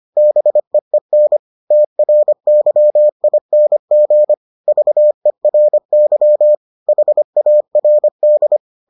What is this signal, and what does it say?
Morse code, heard as a keyed tone, about 25 wpm, 600 Hz. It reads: BEEN TRYING VERY HARD